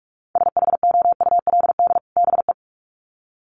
Morse code, heard as a keyed tone, about 44 wpm, 700 Hz. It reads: SHOULD BE
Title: Sets of 2 Words 45wpm